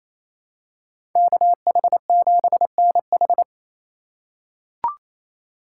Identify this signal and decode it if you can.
Morse code, heard as a keyed tone, about 27 wpm, 700 Hz. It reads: KH7NH
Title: USA Call Signs - 2x Character Spacing - 28wpm